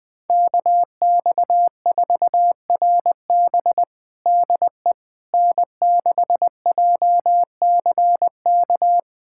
Morse code, heard as a keyed tone, about 20 wpm, 700 Hz. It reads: KX4RB DE N6JCK
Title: USA Callsign DE Callsign - 5 Characters - 20wpm